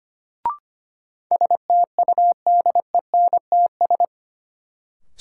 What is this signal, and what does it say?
Morse code, heard as a keyed tone, about 25 wpm, 700 Hz. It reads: STUDENTS
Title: Sets of 5 Words Encore - New Word Drill - 25wpm